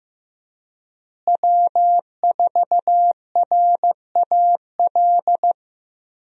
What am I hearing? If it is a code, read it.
Morse code, heard as a keyed tone, about 15 wpm, 700 Hz. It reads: W4RAL
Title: USA Call Signs - 5 Characters - Repeated 2x - 15wpm